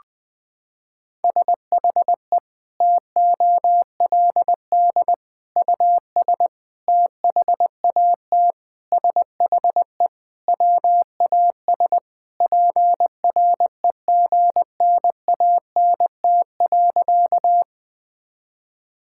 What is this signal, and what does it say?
Morse code, heard as a keyed tone, about 20 wpm, 700 Hz. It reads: SHE TOLD US THAT SHE WAS PREGNANT.
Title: Sentences from Top 1000 Words - Review Part 6 - 20wpm